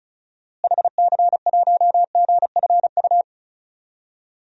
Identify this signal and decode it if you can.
Morse code, heard as a keyed tone, about 36 wpm, 700 Hz. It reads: HC1GFU
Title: DXCC-100 Call Signs - 6 Characters - Repeated 3x - 35wpm